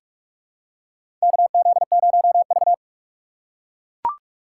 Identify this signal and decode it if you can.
Morse code, heard as a keyed tone, about 43 wpm, 700 Hz. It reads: KZ0V
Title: USA Call Signs - 4 Characters - 6x Character Spacing - 45wpm